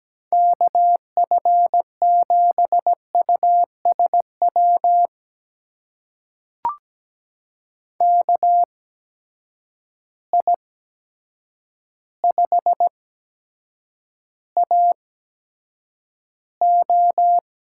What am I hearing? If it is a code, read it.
Morse code, heard as a keyed tone, about 17 wpm, spaced as for 8 wpm, 700 Hz. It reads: KF7USW KI5AO
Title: USA Call Signs - 6 Characters - 8x Character Spacing - 17wpm